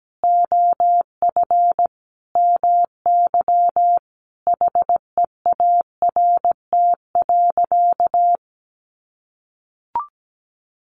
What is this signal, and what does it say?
Morse code, heard as a keyed tone, about 17 wpm, 700 Hz. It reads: OF MY HEART.
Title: Sentences from Top 1300 Words 17wpm